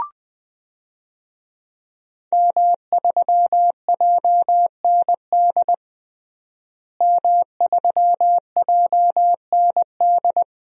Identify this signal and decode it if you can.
Morse code, heard as a keyed tone, about 20 wpm, 700 Hz. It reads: M3JND M3JND
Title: DXCC-100 Call Signs - 5 Characters - Repeated 3x - 20wpm